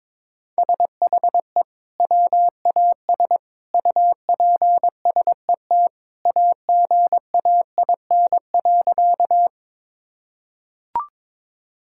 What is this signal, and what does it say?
Morse code, heard as a keyed tone, about 22 wpm, 700 Hz. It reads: SHE WAS UPSET AGAIN.
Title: Sentences from Top 1000 Words - Review Part 5 - 22wpm